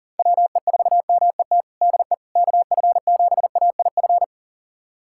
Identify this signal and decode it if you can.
Morse code, heard as a keyed tone, about 37 wpm, 700 Hz. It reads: WE4MET DE KF7AIF
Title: USA Callsign DE Callsign - 6 Characters - 40wpm